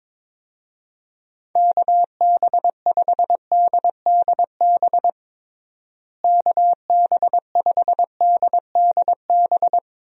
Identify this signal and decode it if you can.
Morse code, heard as a keyed tone, about 22 wpm, 700 Hz. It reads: KB5DDB KB5DDB
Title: USA Call Signs - 6 Characters - Repeated 2x - 22wpm